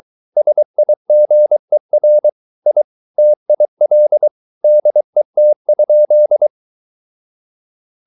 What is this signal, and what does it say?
Morse code, heard as a keyed tone, about 23 wpm, 600 Hz. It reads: SIGER I TIL DET?
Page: Ferie 23 wpm